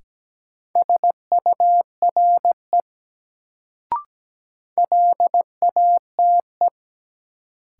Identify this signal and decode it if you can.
Morse code, heard as a keyed tone, about 17 wpm, 700 Hz. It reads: SURE LATE
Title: Sentences from Top 300 Words - New Word Drill - 17wpm